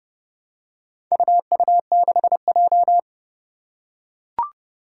Morse code, heard as a keyed tone, about 30 wpm, 700 Hz. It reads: UU6J
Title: Four Challenging Letter-Number Combinations 30wpm